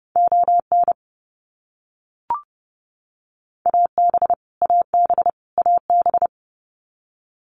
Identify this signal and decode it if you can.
Morse code, heard as a keyed tone, about 30 wpm, 700 Hz. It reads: ON AB AB AB
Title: Canadian Provinces and Territories - Repeated 3x - 30wpm